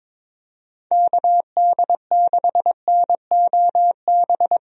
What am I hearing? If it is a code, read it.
Morse code, heard as a keyed tone, about 22 wpm, 700 Hz. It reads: KD6NOB
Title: USA Call Signs - 6 Characters - Repeated 3x - 22wpm